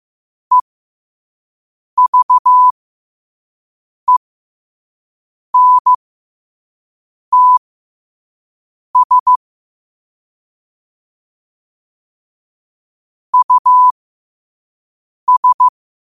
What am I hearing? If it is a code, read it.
Morse code, heard as a keyed tone, about 15 wpm, spaced as for 5 wpm, 1000 Hz. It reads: EVENTS US
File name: qotdfiles / 5wpm